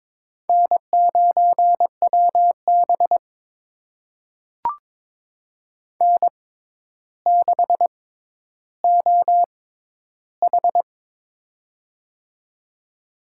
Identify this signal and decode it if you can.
Morse code, heard as a keyed tone, about 22 wpm, spaced as for 11 wpm, 700 Hz. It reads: N9WB N6OH
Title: USA Call Signs - 4 Characters - 6x Character Spacing - 22wpm